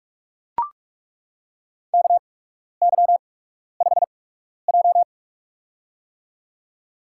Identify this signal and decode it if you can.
Morse code, heard as a keyed tone, about 41 wpm, spaced as for 12 wpm, 700 Hz. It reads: KY5J
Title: USA Call Signs - 4 Characters - 8x Character Spacing - 45wpm